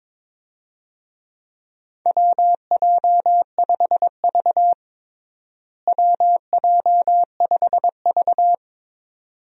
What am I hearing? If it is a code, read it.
Morse code, heard as a keyed tone, about 21 wpm, 700 Hz. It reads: WJ5V WJ5V
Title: USA Call Signs - 4 Characters - Repeated 2x - 22wpm